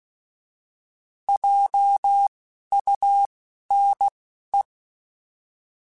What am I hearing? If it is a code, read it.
Morse code, heard as a keyed tone, about 16 wpm, spaced as for 11 wpm, 800 Hz. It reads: JUNE
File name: Alphabetical / Alphabetical mp3s / Js @10